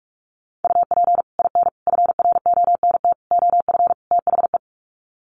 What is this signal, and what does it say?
Morse code, heard as a keyed tone, about 44 wpm, 700 Hz. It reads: UP IN FRONT OF THE
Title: Sets of 5 Words Encore2 45wpm